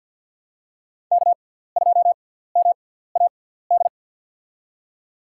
Morse code, heard as a keyed tone, about 52 wpm, spaced as for 17 wpm, 700 Hz. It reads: K2MAD
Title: USA Call Signs - 5 Characters - 6x Character Spacing - 50wpm